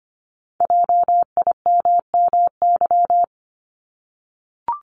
Morse code, heard as a keyed tone, about 25 wpm, 700 Hz. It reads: JIMMY
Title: Multi-letter Phonemes 25wpm